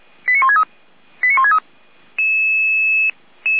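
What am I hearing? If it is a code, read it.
Five-tone selective call (ZVEI), heard as quick runs of five tones, tones of about 70 ms. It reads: 89163 89163
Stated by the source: Five-tone ZVEI selective call received on an SDR, from a fire brigade paging channel.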